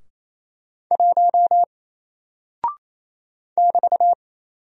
Morse code, heard as a keyed tone, about 27 wpm, 700 Hz. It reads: 1 <BT>
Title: Single Character (Letter:Number:Punctuation:Prosign) 28wpm